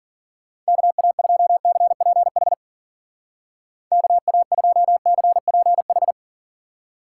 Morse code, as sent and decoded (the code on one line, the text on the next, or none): -.- .- ..--- -.-. .--. .... / -.- .- ..--- -.-. .--. ....
KA2CPH KA2CPH